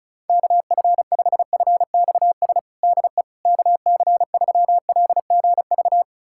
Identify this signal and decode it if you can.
Morse code, heard as a keyed tone, about 34 wpm, 700 Hz. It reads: KF5FXS DE KC3LGV
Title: USA Callsign DE Callsign - 6 Characters - 35wpm